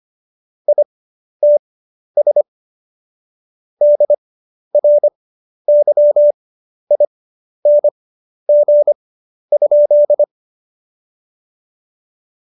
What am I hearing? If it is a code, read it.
Morse code, heard as a keyed tone, about 25 wpm, spaced as for 11 wpm, 590 Hz. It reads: ITS DRYING?
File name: Jokes-25-char-10-eff